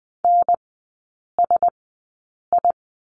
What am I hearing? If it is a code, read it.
Morse code, heard as a keyed tone, about 20 wpm, spaced as for 8 wpm, 700 Hz. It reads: NSI
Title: 074 Lesson 13e - Letter L - Three Characters 20wpm